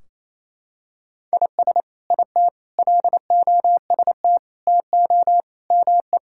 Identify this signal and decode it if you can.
Morse code, heard as a keyed tone, about 28 wpm, 700 Hz. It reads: IS IT LOST TO ME
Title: Sentences from Top 1600 Words 28wpm